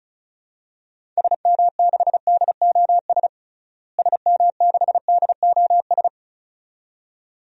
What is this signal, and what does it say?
Morse code, heard as a keyed tone, about 36 wpm, 700 Hz. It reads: SM6DOS SM6DOS
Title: DXCC-100 Call Signs - 6 Characters - Repeated 2x - 35wpm